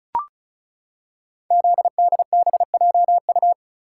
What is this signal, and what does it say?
Morse code, heard as a keyed tone, about 36 wpm, 700 Hz. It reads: ZDBJU
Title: Code Groups - 5 Letters - 35wpm